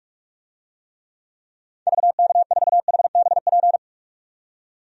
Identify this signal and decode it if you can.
Morse code, heard as a keyed tone, about 46 wpm, 700 Hz. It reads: VK4HBP
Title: DXCC-100 Call Signs - 6 Characters - 45wpm